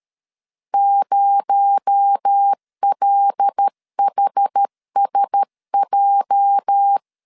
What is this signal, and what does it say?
Morse code, heard as a keyed tone, about 13 wpm, 800 Hz. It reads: 0LHSJ
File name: LEZIONI / 9 28 12wpm 2351